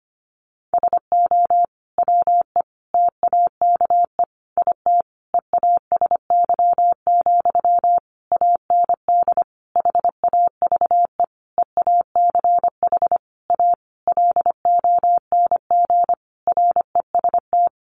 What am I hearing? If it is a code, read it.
Morse code, heard as a keyed tone, about 25 wpm, 700 Hz. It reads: SO WE TAKE IT EASY, AND HAVE EACH A LONG REST
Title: Sentences from Top 700 Words 25wpm